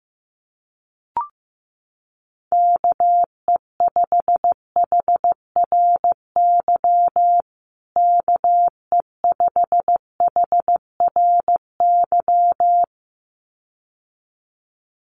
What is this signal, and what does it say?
Morse code, heard as a keyed tone, about 15 wpm, 700 Hz. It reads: KE5HRY KE5HRY
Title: USA Call Signs Repeated 2x 15wpm